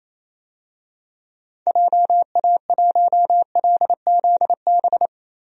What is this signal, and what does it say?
Morse code, heard as a keyed tone, about 28 wpm, 700 Hz. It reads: JA1LZB
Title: DXCC-100 Call Signs - 6 Characters - Repeated 2x - 28wpm